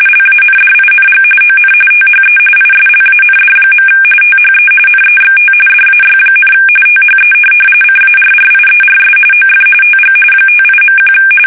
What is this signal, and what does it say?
RTTY, about 45.45 baud. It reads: LANDING ON THE ISLAND. THE NEXT DAY, THE CARRIERS OF TG 58.1
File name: fsk850 / USS-Massachusetts